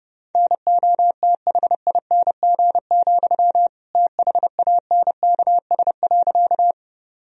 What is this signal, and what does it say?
Morse code, heard as a keyed tone, about 32 wpm, 700 Hz. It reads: NOTHING, THANKS.
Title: Sentences from Top 1000 Words - Review Part 3 - 30wpm